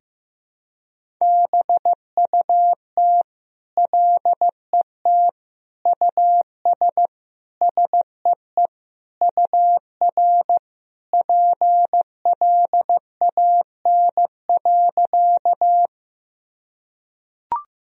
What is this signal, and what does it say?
Morse code, heard as a keyed tone, about 15 wpm, 700 Hz. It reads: BUT LET US SEE UR PLAN.
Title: Sentences from Top 500 Words and Abbreviations 15wpm